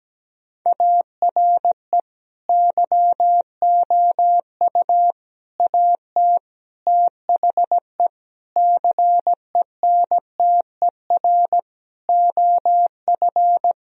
Morse code, heard as a keyed tone, about 17 wpm, 700 Hz. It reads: ARE YOU AT THE CENTER OF
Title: Sentences from Top 1000 Words - Review Part 2 - 17wpm